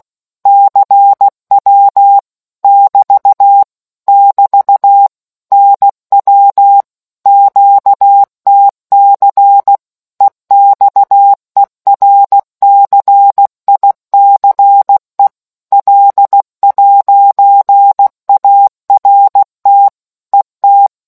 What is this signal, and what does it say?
Morse code, heard as a keyed tone, about 16 wpm, 800 Hz. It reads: CW <BT> <BT> NW QTC EXERCICE L'ART ET